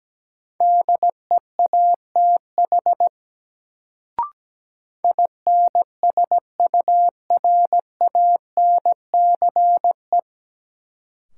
Morse code, heard as a keyed tone, about 17 wpm, 700 Hz. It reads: DEATH INSURANCE
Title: Sentences from Top 1100 Words - New Word Drill - 17wpm